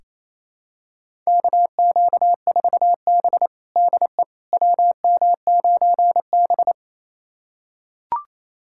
Morse code, heard as a keyed tone, about 28 wpm, 700 Hz. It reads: KQ4B DE WM9B
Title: USA Callsign DE Callsign - 4 Characters - 28wpm